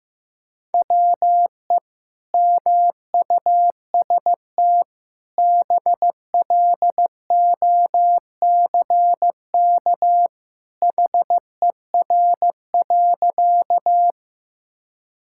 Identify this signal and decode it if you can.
Morse code, heard as a keyed tone, about 15 wpm, 700 Hz. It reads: WE MUST BLOCK HER.
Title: Sentences from Top 1600 Words 15wpm